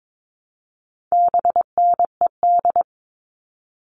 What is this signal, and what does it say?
Morse code, heard as a keyed tone, about 22 wpm, 700 Hz. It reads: BNED
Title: All Cut Numbers - 4-Digit 22wpm